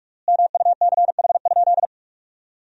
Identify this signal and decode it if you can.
Morse code, heard as a keyed tone, about 46 wpm, 700 Hz. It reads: MUCH?
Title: 3x Word-Spacing - Sentences from Top 400 Words - 45wpm